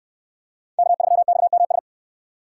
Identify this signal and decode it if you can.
Morse code, heard as a keyed tone, about 69 wpm, 700 Hz. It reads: K3XMS